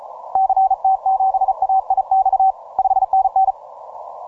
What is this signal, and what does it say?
Morse code, heard as a keyed tone, about 34 wpm, 770 Hz. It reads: CT7AIX 5NN